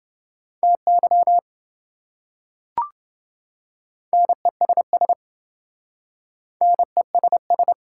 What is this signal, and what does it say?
Morse code, heard as a keyed tone, about 27 wpm, 700 Hz. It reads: TY NESS NESS